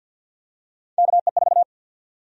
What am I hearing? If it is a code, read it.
Morse code, heard as a keyed tone, about 52 wpm, 700 Hz. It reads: KE4
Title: USA Call Sign Prefixes - 50wpm